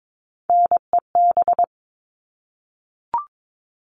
Morse code, heard as a keyed tone, about 22 wpm, 700 Hz. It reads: NEB
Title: All Cut Numbers - 3-Digit 22wpm